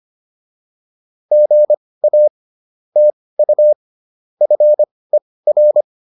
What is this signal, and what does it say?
Morse code, heard as a keyed tone, about 25 wpm, spaced as for 18 wpm, 600 Hz. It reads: GA TU FER
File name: QSO104 18